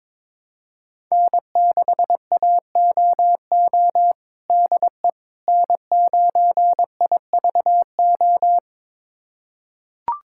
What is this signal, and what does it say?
Morse code, heard as a keyed tone, about 22 wpm, 700 Hz. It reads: N6AOO DE N9IVO
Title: USA Callsign DE Callsign - 5 Characters - 22wpm